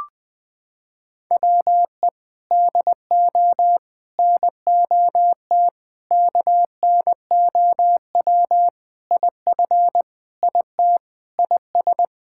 Morse code, heard as a keyed tone, about 20 wpm, 700 Hz. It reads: WE DO NOT KNOW IF IT IS
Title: Sentences with 4-Word Phrases - 20wpm